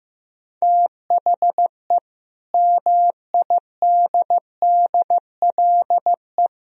Morse code, heard as a keyed tone, about 15 wpm, 700 Hz. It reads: THE MIDDLE
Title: Speed-Racing - Sets of 5 Words 15wpm